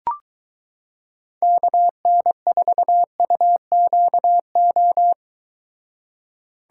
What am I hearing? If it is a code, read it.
Morse code, heard as a keyed tone, about 23 wpm, 700 Hz. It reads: KN4UQO